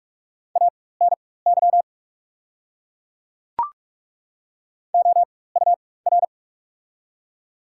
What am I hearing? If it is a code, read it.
Morse code, heard as a keyed tone, about 43 wpm, spaced as for 21 wpm, 690 Hz. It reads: ANY OUR
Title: 4x Character Spacing - Top 200 Words - 45wpm